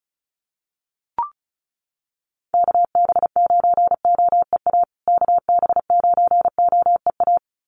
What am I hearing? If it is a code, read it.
Morse code, heard as a keyed tone, about 34 wpm, 700 Hz. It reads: KB9OEA KB9OEA
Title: USA Call Signs Repeated 2x 35wpm